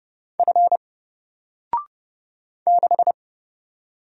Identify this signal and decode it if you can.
Morse code, heard as a keyed tone, about 30 wpm, 700 Hz. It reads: F 6
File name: Single-Letter-Number-30wpm